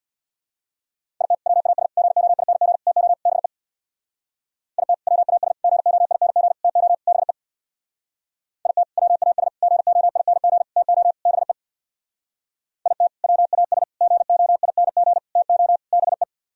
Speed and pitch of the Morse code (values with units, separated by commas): 64 wpm, 700 Hz